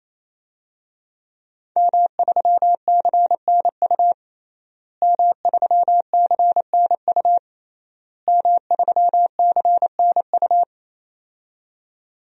Morse code, heard as a keyed tone, about 28 wpm, 700 Hz. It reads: M3CNU M3CNU M3CNU
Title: DXCC-100 Call Signs - 5 Characters - Repeated 3x - 28wpm